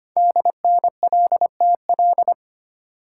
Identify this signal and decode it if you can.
Morse code, heard as a keyed tone, about 25 wpm, 700 Hz. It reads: DNLTL
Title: Code Groups - 5 Letters - 25wpm